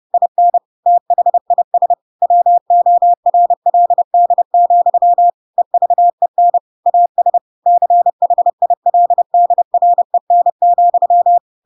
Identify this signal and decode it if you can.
Morse code, heard as a keyed tone, about 30 wpm, 700 Hz. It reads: IN THIS WORLD, EVEN AS CHILDREN,
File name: Pinocchio / Pinocchio ch7 30 30 ik1hge